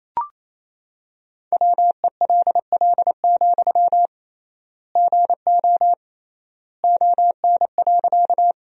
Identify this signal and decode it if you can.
Morse code, heard as a keyed tone, about 28 wpm, 700 Hz. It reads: WELL, GO ON.
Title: 3x Word-Spacing - Sentences from Top 200 Words - 28wpm